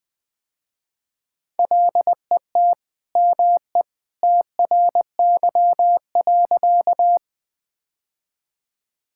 Morse code, heard as a keyed tone, about 20 wpm, 700 Hz. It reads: LET ME TRY.
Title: Sentences with 2-Word Phrases - Spoken Beginning - 20wpm